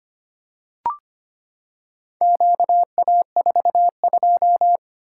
Morse code, heard as a keyed tone, about 25 wpm, 700 Hz. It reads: QA42
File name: practice / 25wpm / Four Letter-Number Combinations - 25wpm